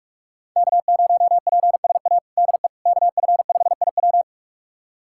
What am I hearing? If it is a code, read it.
Morse code, heard as a keyed tone, about 46 wpm, 700 Hz. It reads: K0PSA DE KF5IW
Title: USA Callsign DE Callsign - 5 Characters - 45wpm